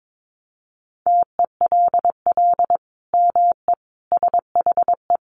Morse code, heard as a keyed tone, about 22 wpm, 700 Hz. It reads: TELL ME SHE
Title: Sentences from Top 1200 Words 22wpm